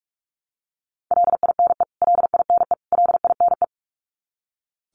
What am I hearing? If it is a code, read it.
Morse code, heard as a keyed tone, about 44 wpm, 700 Hz. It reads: LINE LINE LINE
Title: Top 200 Words Repeated 45wpm